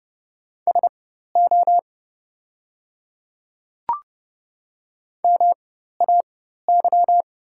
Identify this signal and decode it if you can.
Morse code, heard as a keyed tone, about 30 wpm, spaced as for 14 wpm, 690 Hz. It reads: SO MAY
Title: 4x Character Spacing - Top 100 Words - 30wpm